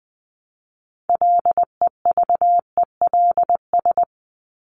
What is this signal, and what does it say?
Morse code, heard as a keyed tone, about 20 wpm, 700 Hz. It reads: LEVELS
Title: ES:S Suffix - Top 100 Words with Base Words 20wpm